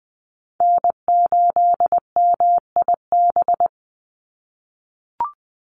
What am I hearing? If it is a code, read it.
Morse code, heard as a keyed tone, about 20 wpm, 700 Hz. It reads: N8MIB
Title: USA Call Signs 20wpm